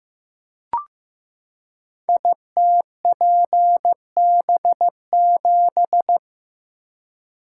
Morse code, heard as a keyed tone, about 15 wpm, 700 Hz. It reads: ITPB7
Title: Code Groups - 5 Letter:Numbers - 15wpm